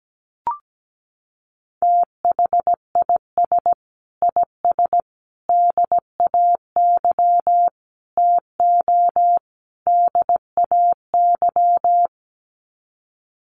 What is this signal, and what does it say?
Morse code, heard as a keyed tone, about 17 wpm, 700 Hz. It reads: THIS IS DAY TO DAY
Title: Sets of 5 Words Encore 17wpm